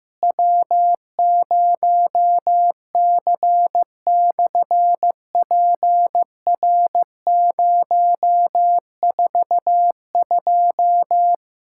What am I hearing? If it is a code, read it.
Morse code, heard as a keyed tone, about 15 wpm, 700 Hz. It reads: W0C/PR042
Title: SOTA QSO Practice 15wpm